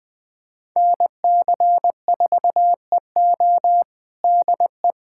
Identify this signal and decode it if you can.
Morse code, heard as a keyed tone, about 20 wpm, 700 Hz. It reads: NC4EO DE
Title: USA Callsign DE Callsign 20wpm